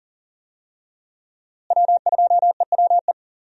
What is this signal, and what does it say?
Morse code, heard as a keyed tone, about 38 wpm, 700 Hz. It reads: W2EWE
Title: USA Call Signs - 5 Characters - Repeated 3x - 40wpm